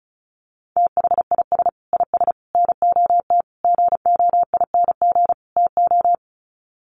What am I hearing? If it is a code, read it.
Morse code, heard as a keyed tone, about 34 wpm, 700 Hz. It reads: THIS IS NOT GOING TO
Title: Sets of 5 Words Encore2 35wpm